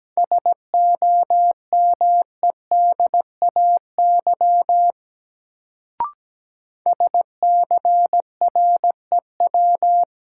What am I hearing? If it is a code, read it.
Morse code, heard as a keyed tone, about 17 wpm, 700 Hz. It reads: SOMEDAY SCREW
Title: Sentences from Top 1600 Words - New Word Drill - 17wpm